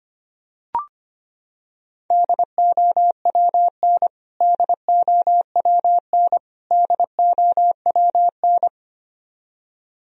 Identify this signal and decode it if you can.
Morse code, heard as a keyed tone, about 25 wpm, 700 Hz. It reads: DOWN DOWN DOWN
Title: Top 100 Words Repeated 25wpm